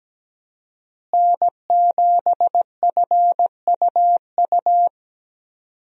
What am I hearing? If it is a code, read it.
Morse code, heard as a keyed tone, about 17 wpm, 700 Hz. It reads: N7FUU
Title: USA Call Signs - 5 Characters - Repeated 3x - 17wpm